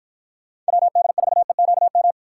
Morse code, heard as a keyed tone, about 50 wpm, 700 Hz. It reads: WD4EQM